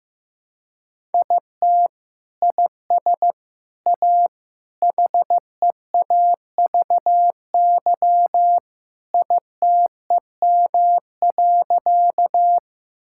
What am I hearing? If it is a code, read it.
Morse code, heard as a keyed tone, about 16 wpm, 700 Hz. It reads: IT IS A HEAVY ITEM.
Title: Sentences from Top 1000 Words - Review Part 6 - 15wpm